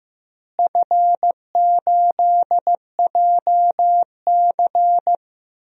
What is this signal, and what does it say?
Morse code, heard as a keyed tone, about 15 wpm, 700 Hz. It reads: F8JC
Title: DXCC-100 Call Signs - 4 Characters - Repeated 2x - 15wpm